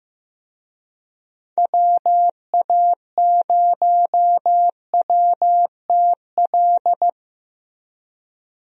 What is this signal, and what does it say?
Morse code, heard as a keyed tone, about 15 wpm, 700 Hz. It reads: WA0WTL
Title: USA Call Signs - 6 Characters - Repeated 2x - 15wpm